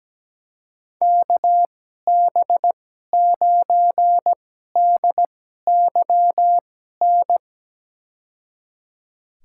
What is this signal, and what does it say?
Morse code, heard as a keyed tone, about 17 wpm, spaced as for 12 wpm, 700 Hz. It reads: KB9DYN
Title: USA Call Signs - 6 Characters - 2x Character Spacing - 17wpm